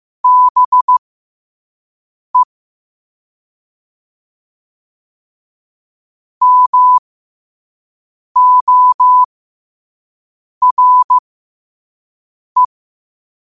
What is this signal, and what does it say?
Morse code, heard as a keyed tone, about 15 wpm, spaced as for 5 wpm, 1000 Hz. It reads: BE MORE
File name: qotdfiles / 5wpm